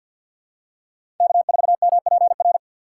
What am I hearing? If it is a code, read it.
Morse code, heard as a keyed tone, about 52 wpm, 700 Hz. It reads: K4MPR
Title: USA Call Signs - 5 Characters - 4x Character Spacing - 50wpm